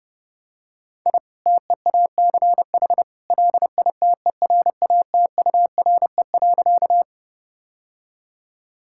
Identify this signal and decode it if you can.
Morse code, heard as a keyed tone, about 30 wpm, 700 Hz. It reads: I TEACH LITERATURE.
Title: Sentences from Top 1000 Words - Review Part 6 - 30wpm